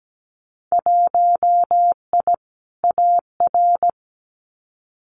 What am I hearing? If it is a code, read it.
Morse code, heard as a keyed tone, about 17 wpm, 700 Hz. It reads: 1I AR
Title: Winter Field Day Exchanges Encore 17wpm